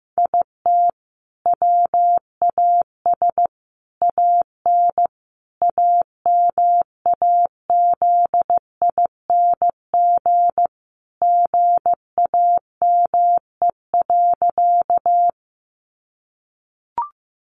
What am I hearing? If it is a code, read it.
Morse code, heard as a keyed tone, about 15 wpm, 700 Hz. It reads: IT WAS AN AMAZING GAME.
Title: Sentences from Top 1500 Words 15wpm